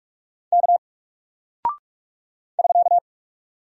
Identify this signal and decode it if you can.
Morse code, heard as a keyed tone, about 44 wpm, 700 Hz. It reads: K <SK>
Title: Single Character (Letter:Number:Punctuation:Prosign) Alt 45wpm